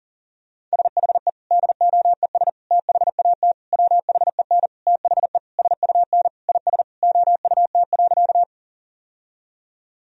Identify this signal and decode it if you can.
Morse code, heard as a keyed tone, about 40 wpm, 700 Hz. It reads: SHE DOES THAT WHEN THE SUN IS OUT.
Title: Sentences from Top 1000 Words - Review Part 1 - 40wpm